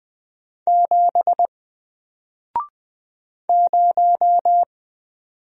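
Morse code, heard as a keyed tone, about 20 wpm, 700 Hz. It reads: 7 0
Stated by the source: Single letter and number Morse drill at 20 wpm, from N4LCD.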